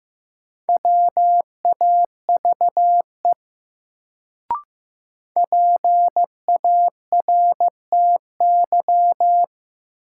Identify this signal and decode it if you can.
Morse code, heard as a keyed tone, about 15 wpm, 700 Hz. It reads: WAVE PARTY
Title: Sentences from Top 1400 Words - New Word Drill Alt - 15wpm